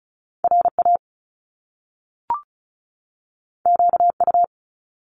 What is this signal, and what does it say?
Morse code, heard as a keyed tone, about 36 wpm, 700 Hz. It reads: RA QU
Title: Top Two Letter Combinations in Words 35wpm